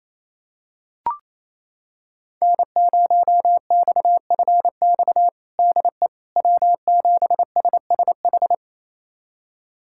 Morse code, heard as a keyed tone, about 27 wpm, 700 Hz. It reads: N0XFX DE W7SSH
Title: USA Callsign DE Callsign - 5 Characters - 28wpm